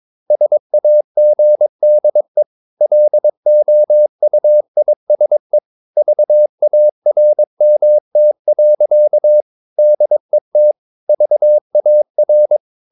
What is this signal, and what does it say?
Morse code, heard as a keyed tone, about 22 wpm, 600 Hz. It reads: SAGDE LOUISE VARMT. DET VAR
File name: Ferie QRS 22wpm